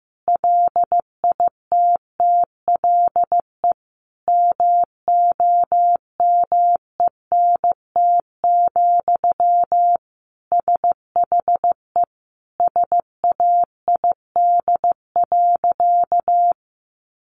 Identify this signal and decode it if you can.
Morse code, heard as a keyed tone, about 14 wpm, 700 Hz. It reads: LITTLE MOMENT, SHE SAID.
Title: Sentences from Top 1000 Words 15wpm